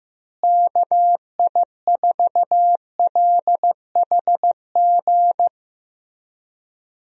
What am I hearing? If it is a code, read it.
Morse code, heard as a keyed tone, about 15 wpm, 700 Hz. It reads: KI4LHG
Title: USA Call Signs - 6 Characters - Repeated 2x - 15wpm